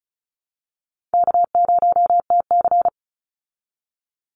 Morse code, heard as a keyed tone, about 34 wpm, 700 Hz. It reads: K0TC
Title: Easy Call Signs Repeated 3x - 35wpm